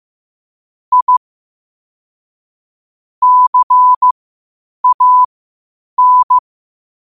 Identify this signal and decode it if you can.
Morse code, heard as a keyed tone, about 15 wpm, spaced as for 8 wpm, 1000 Hz. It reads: I CAN
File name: qotdfiles / 8wpm